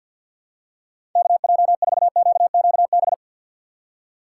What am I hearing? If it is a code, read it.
Morse code, heard as a keyed tone, about 47 wpm, 700 Hz. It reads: KJ4QQB
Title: USA Call Signs - 6 Characters - 6x Character Spacing - 50wpm